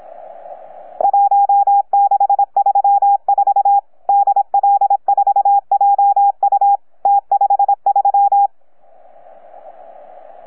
Morse code, heard as a keyed tone, about 26 wpm, 810 Hz. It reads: 1634 DL4JU T53
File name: UP2L-20-CW